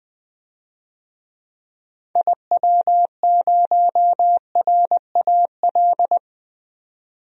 Morse code, heard as a keyed tone, about 20 wpm, 700 Hz. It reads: IW0RAL
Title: DXCC-100 Call Signs - 6 Characters - 20wpm